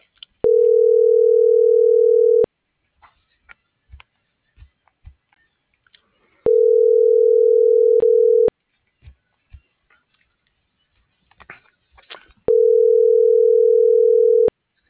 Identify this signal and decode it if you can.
Ringback tone (North America), 440 + 480 Hz.
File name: CENTRAL SAGOT / out edgar.sagot